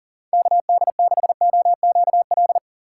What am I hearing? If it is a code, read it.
Morse code, heard as a keyed tone, about 37 wpm, 700 Hz. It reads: KD6OQL
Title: USA Call Signs - 6x Character Spacing - 40wpm